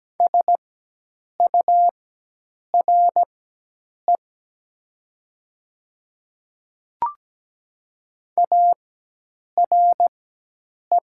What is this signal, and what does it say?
Morse code, heard as a keyed tone, about 17 wpm, spaced as for 8 wpm, 690 Hz. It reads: SURE ARE
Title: 4x Character Spacing - Top 300 Words - 17wpm